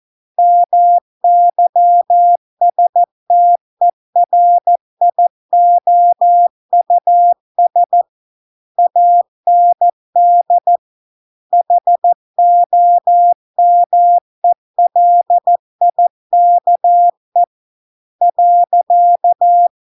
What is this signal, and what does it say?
Morse code, heard as a keyed tone, about 14 wpm, 700 Hz. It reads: MYSTERIOUS AND HOMELIKE .